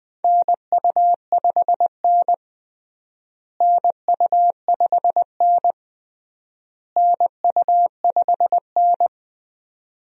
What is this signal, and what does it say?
Morse code, heard as a keyed tone, about 20 wpm, 700 Hz. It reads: NU5N NU5N NU5N
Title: USA Call Signs - 4 Characters - Repeated 3x - 20wpm